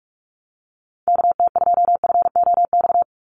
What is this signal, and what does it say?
Morse code, heard as a keyed tone, about 44 wpm, 700 Hz. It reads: KT2FOX
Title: USA Call Signs 45wpm